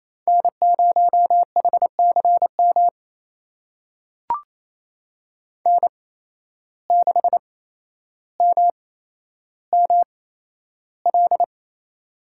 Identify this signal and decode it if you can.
Morse code, heard as a keyed tone, about 28 wpm, spaced as for 12 wpm, 700 Hz. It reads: N0HCM N6MML
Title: USA Call Signs - 5 Characters - 8x Character Spacing - 28wpm